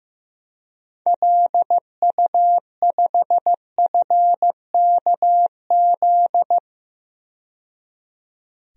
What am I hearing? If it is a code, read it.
Morse code, heard as a keyed tone, about 15 wpm, 700 Hz. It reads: LU5FKZ
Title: DXCC-100 Call Signs - 6 Characters - 15wpm